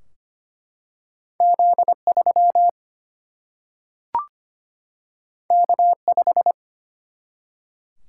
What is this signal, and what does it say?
Morse code, heard as a keyed tone, about 25 wpm, 700 Hz. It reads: Z3 K5
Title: Single-Letter Single-Number 25wpm